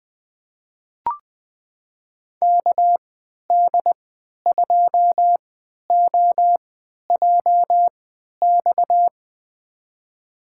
Morse code, heard as a keyed tone, about 20 wpm, spaced as for 11 wpm, 700 Hz. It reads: KD2OJX
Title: USA Call Signs - 3x Character Spacing - 20wpm